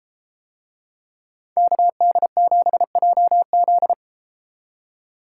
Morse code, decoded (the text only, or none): KD7JZ